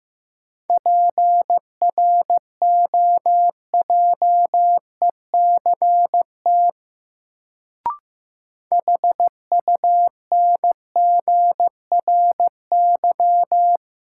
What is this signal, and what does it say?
Morse code, heard as a keyed tone, about 15 wpm, 700 Hz. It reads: PROJECT HUNGRY
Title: Sentences from Top 1400 Words - New Word Drill - 15wpm